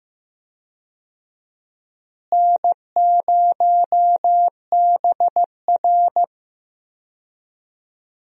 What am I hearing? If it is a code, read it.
Morse code, heard as a keyed tone, about 16 wpm, 700 Hz. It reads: N0BR
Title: USA Call Signs - 4 Characters - Repeated 3x - 15wpm